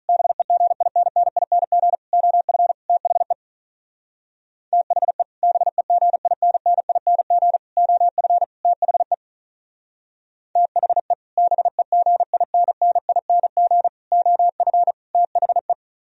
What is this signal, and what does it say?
Morse code, heard as a keyed tone, about 46 wpm, 700 Hz. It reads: BEGINNING OF THE THE BEGINNING OF THE THE BEGINNING OF THE